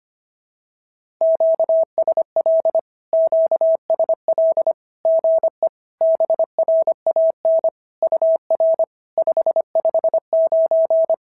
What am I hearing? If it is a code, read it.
Morse code, heard as a keyed tone, about 25 wpm, 650 Hz. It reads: QSL QSL GE BRAN UR 559
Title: SOTA QSO Practice 25wpm